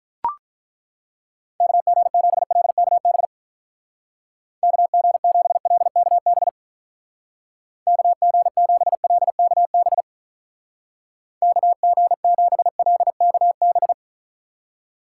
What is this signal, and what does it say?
Morse code, heard as a keyed tone, about 43 wpm, 700 Hz. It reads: KG7LKB KG7LKB KG7LKB KG7LKB
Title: Speed Racing - USA 2x3 Call Signs 35wpm